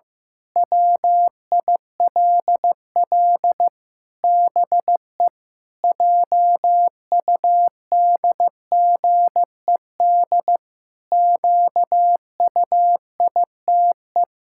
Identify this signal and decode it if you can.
Morse code, heard as a keyed tone, about 15 wpm, 700 Hz. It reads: WILL BE JUDGED QUITE
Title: Pangram Challenge v2 - 15wpm